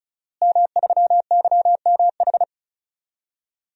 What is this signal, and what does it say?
Morse code, heard as a keyed tone, about 36 wpm, 700 Hz. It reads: M3YMH
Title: DXCC-100 Call Signs - 5 Characters - Repeated 2x - 35wpm